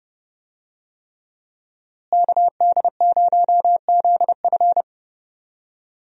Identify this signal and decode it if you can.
Morse code, heard as a keyed tone, about 30 wpm, 700 Hz. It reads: KD0ZF
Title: USA Call Signs - 5 Characters - Repeated 3x - 30wpm